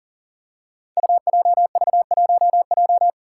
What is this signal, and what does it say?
Morse code, heard as a keyed tone, about 43 wpm, 700 Hz. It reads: UJV1J
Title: Five Challenging Letter-Number Combinations 40wpm